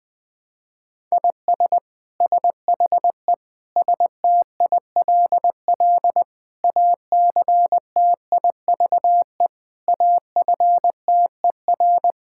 Morse code, heard as a keyed tone, about 20 wpm, 700 Hz. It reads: IS SHE STILL ACTIVE AFTER
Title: Sentences from Top 1000 Words - Review Part 6 - 20wpm